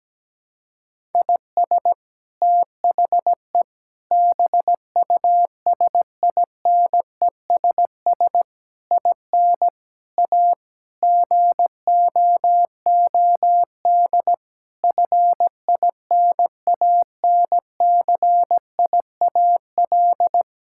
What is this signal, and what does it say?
Morse code, heard as a keyed tone, about 17 wpm, 700 Hz. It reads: IS THE BUSINESS IN A GOOD FINANCIAL
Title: Sentences from Top 1000 Words - Review Part 5 - 17wpm